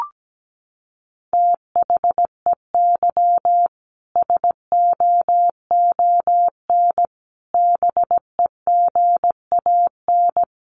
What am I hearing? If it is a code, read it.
Morse code, heard as a keyed tone, about 17 wpm, 700 Hz. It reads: THEY SOON BEGAN
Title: Sentences from Top 1300 Words 17wpm